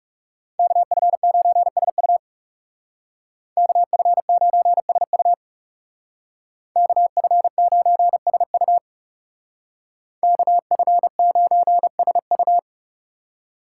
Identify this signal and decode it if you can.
Morse code, heard as a keyed tone, about 40 wpm, 700 Hz. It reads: KF9SU KF9SU KF9SU KF9SU